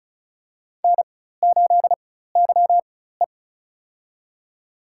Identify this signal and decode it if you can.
Morse code, heard as a keyed tone, about 36 wpm, spaced as for 16 wpm, 700 Hz. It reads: N8YE
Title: USA Call Signs - 4x Character Spacing - 35wpm